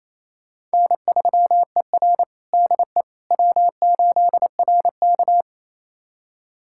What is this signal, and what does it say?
Morse code, heard as a keyed tone, about 28 wpm, 700 Hz. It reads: N3ER DE W8RK
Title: USA Callsign DE Callsign - 4 Characters - 28wpm